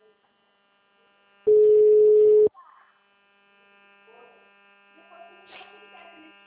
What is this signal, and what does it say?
Ringback tone (Europe, 425 Hz).